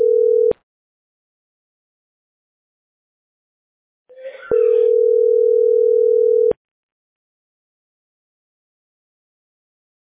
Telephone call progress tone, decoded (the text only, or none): ringback tone